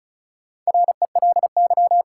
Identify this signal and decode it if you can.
Morse code, heard as a keyed tone, about 36 wpm, 700 Hz. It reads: RELY
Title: Sentences from Top 1200 Words - New Word Drill Alt - 35wpm